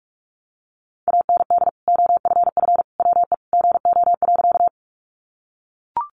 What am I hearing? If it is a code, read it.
Morse code, heard as a keyed tone, about 44 wpm, 700 Hz. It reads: AND OFF WE GO.
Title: Sentences from Top 200 Words 45wpm